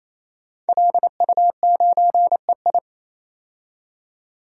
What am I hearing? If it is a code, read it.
Morse code, heard as a keyed tone, about 28 wpm, 700 Hz. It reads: LU9EI
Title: DXCC-100 Call Signs - 5 Characters - 28wpm